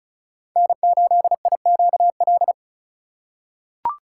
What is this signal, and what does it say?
Morse code, heard as a keyed tone, about 36 wpm, 700 Hz. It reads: N8IQL
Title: USA Call Signs - 5 Characters - 2x Character Spacing - 35wpm